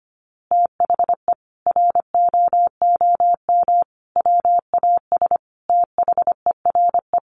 Morse code, heard as a keyed tone, about 25 wpm, 700 Hz. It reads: THE ROOM WAS THERE
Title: A Sound Of Thunder 25wpm